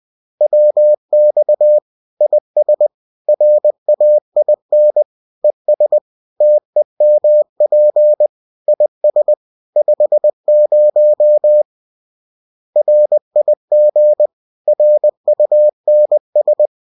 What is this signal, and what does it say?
Morse code, heard as a keyed tone, about 20 wpm, 600 Hz. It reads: WX IS RAIN ES TEMP IS 50 RIG RUNS